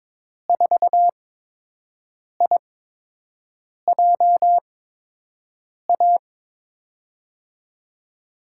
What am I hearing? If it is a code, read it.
Morse code, heard as a keyed tone, about 22 wpm, spaced as for 6 wpm, 700 Hz. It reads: 4IJA
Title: USA Call Signs - 6 Characters - 8x Character Spacing - 22wpm